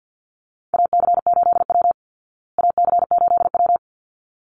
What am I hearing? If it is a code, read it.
Morse code, heard as a keyed tone, about 48 wpm, 700 Hz. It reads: AC8W AC8W
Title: Call Signs Repeated 3x - 50wpm